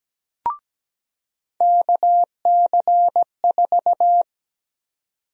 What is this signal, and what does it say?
Morse code, heard as a keyed tone, about 17 wpm, 700 Hz. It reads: KC4
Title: USA Call Sign Prefixes - 17wpm